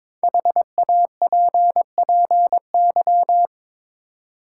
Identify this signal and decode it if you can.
Morse code, heard as a keyed tone, about 22 wpm, 700 Hz. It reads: HAPPY
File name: 5 Letter Words - First 3 Letters Given - 22wpm